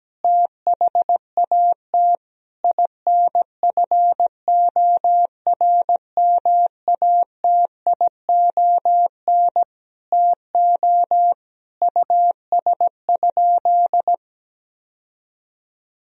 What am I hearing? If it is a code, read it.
Morse code, heard as a keyed tone, about 17 wpm, 700 Hz. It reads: THAT INFORMATION TO US?
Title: Sentences from Top 1000 Words - Review Part 3 - 17wpm